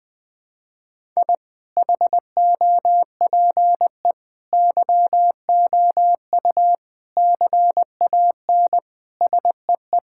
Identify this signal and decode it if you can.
Morse code, heard as a keyed tone, about 20 wpm, 700 Hz. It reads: I HOPE YOU CAN SEE
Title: Sets of 9 Words - 20wpm